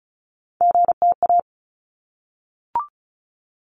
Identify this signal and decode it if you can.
Morse code, heard as a keyed tone, about 34 wpm, 700 Hz. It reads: GTA
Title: ARRL Sections 35wpm